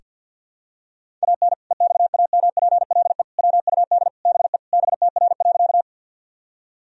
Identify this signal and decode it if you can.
Morse code, heard as a keyed tone, about 46 wpm, 700 Hz. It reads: AN EXAMPLE WUD BE BTR.
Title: Sentences from Top 400 Words and Abbreviations 50wpm